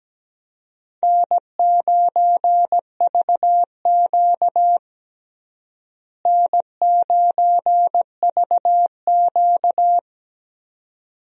Morse code, heard as a keyed tone, about 17 wpm, 700 Hz. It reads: N9VQ N9VQ
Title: USA Call Signs - 4 Characters - Repeated 2x - 17wpm